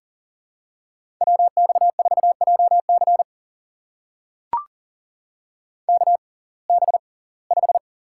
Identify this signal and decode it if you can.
Morse code, heard as a keyed tone, about 40 wpm, spaced as for 24 wpm, 700 Hz. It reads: WX4JC KB5
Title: USA Call Signs - 6x Character Spacing - 40wpm